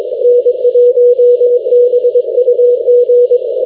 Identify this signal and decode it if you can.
Morse code, heard as a keyed tone, about 21 wpm, 490 Hz. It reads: N1BUG
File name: v51as- -x